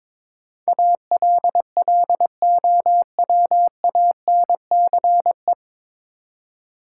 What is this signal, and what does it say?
Morse code, heard as a keyed tone, about 22 wpm, 700 Hz. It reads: ALLOWANCE
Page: Morse Code Ninja practice 22wpm